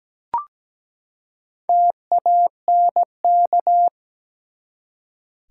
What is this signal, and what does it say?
Morse code, heard as a keyed tone, about 17 wpm, 700 Hz. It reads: TANK
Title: Four Letter Words - 17wpm